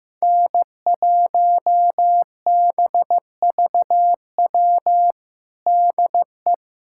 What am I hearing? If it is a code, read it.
Morse code, heard as a keyed tone, about 15 wpm, 700 Hz. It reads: N1BVW DE
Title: USA Callsign DE Callsign - 5 Characters - 15wpm